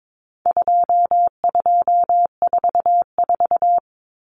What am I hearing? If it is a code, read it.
Morse code, heard as a keyed tone, about 22 wpm, 700 Hz. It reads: 2244
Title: Common Cut Numbers - 4-Digit 22wpm